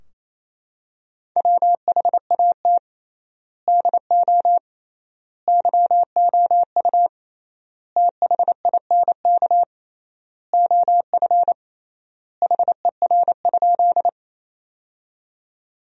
Morse code, heard as a keyed tone, about 28 wpm, 700 Hz. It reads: WHAT DO YOU THINK OF HER?
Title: 3x Word-Spacing - Sentences from Top 200 Words - 28wpm